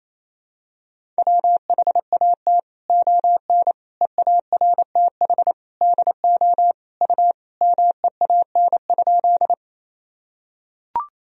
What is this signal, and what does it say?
Morse code, heard as a keyed tone, about 28 wpm, 700 Hz. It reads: WHAT ON EARTH DO U MEAN?
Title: Sentences from Top 200 Words and Abbreviations 28wpm